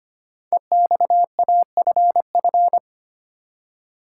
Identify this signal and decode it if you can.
Morse code, heard as a keyed tone, about 25 wpm, 700 Hz. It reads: EXAFF
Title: Code Groups - 5 Letters - 25wpm